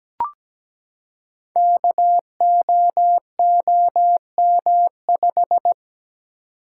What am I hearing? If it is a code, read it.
Morse code, heard as a keyed tone, about 17 wpm, 700 Hz. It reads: KOOM5
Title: Code Groups - 5 Letter:Numbers - 17wpm